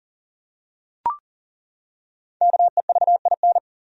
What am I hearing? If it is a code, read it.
Morse code, heard as a keyed tone, about 38 wpm, 700 Hz. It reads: KEVIN
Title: USA Common Names - Max 5 Letters - 40wpm